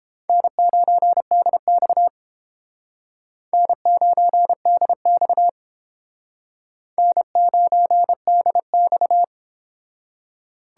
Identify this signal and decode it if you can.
Morse code, heard as a keyed tone, about 30 wpm, 700 Hz. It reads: N9DX N9DX N9DX